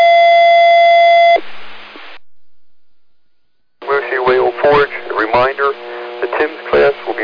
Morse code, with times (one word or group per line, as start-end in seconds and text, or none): none